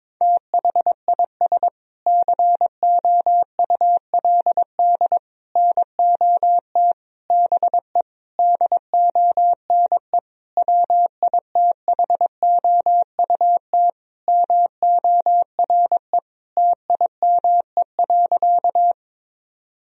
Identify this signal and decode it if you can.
Morse code, heard as a keyed tone, about 22 wpm, 700 Hz. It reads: THIS COULD NOT BE DONE WITHOUT MORE TIME.
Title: Sentences from Top 1000 Words - Review Part 3 - 22wpm